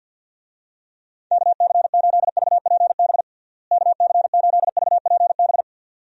Morse code, heard as a keyed tone, about 47 wpm, 700 Hz. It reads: KC8VPB KC8VPB
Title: USA Call Signs - 6 Characters - Repeated 2x - 50wpm